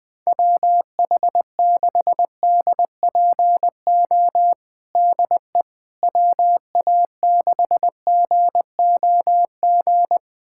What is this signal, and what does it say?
Morse code, heard as a keyed tone, about 20 wpm, 700 Hz. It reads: WH6DPO DE WA6GOG
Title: USA Callsign DE Callsign 20wpm